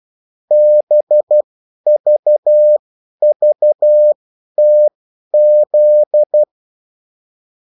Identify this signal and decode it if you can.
Morse code, heard as a keyed tone, about 12 wpm, spaced as for 10 wpm, 600 Hz. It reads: BVVTZ
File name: morse mp3 / revision9-vitesse 10 mots